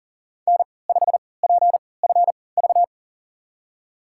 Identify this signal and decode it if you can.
Morse code, heard as a keyed tone, about 40 wpm, spaced as for 23 wpm, 700 Hz. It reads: N5PFV
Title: USA Call Signs - 5 Characters - 3x Character Spacing - 40wpm